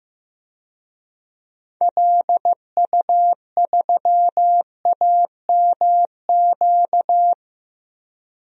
Morse code, heard as a keyed tone, about 15 wpm, 700 Hz. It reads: LU3AMQ
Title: DXCC-100 Call Signs - 6 Characters - 15wpm